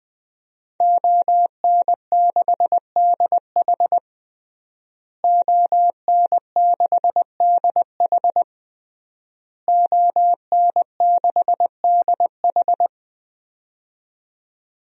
Morse code, heard as a keyed tone, about 21 wpm, 700 Hz. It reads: ON6DH ON6DH ON6DH
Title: DXCC-100 Call Signs - 5 Characters - Repeated 3x - 20wpm